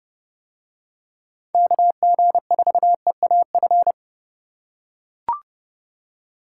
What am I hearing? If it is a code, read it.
Morse code, heard as a keyed tone, about 29 wpm, 700 Hz. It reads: KG4EAF